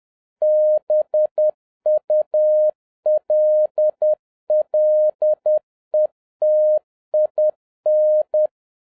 Morse code, heard as a keyed tone, about 10 wpm, 610 Hz. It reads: BULLETIN
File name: 2011-07-mp3 / n36 THE 3.11 2 10wpm